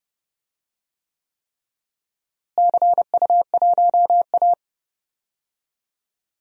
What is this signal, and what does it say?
Morse code, heard as a keyed tone, about 32 wpm, 700 Hz. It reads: CU1A
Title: DXCC-100 Call Signs - 4 Characters - Repeated 3x - 30wpm